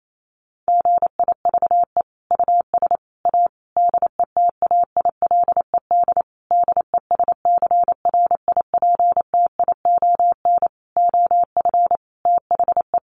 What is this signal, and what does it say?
Morse code, heard as a keyed tone, about 28 wpm, 700 Hz. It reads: GIVE US A DETAILED DESCRIPTION OF THE
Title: Sentences from Top 1500 Words 28wpm